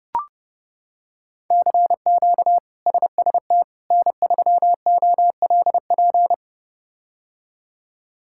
Morse code, heard as a keyed tone, about 31 wpm, 700 Hz. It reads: CQ SST N3OLP
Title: K1USN SST QSO Practice - 30wpm